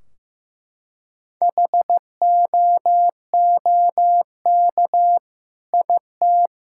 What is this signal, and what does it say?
Morse code, heard as a keyed tone, about 14 wpm, 700 Hz. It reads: HOOK IT
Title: Sentences from Top 1600 Words 15wpm